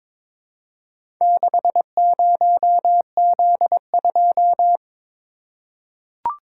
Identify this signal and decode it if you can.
Morse code, heard as a keyed tone, about 23 wpm, 700 Hz. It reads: 60Z2
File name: practice / 22wpm / Four Letter-Number Combinations - 22wpm